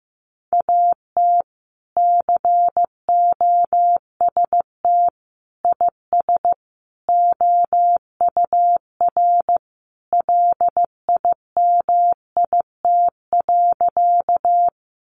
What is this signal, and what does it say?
Morse code, heard as a keyed tone, about 15 wpm, 700 Hz. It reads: AT COST IS OUR LIMIT.
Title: Sentences from Top 1500 Words 15wpm